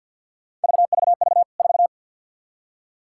Morse code, heard as a keyed tone, about 51 wpm, 700 Hz. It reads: VVV 4
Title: K1USN SST QSO Practice 50wpm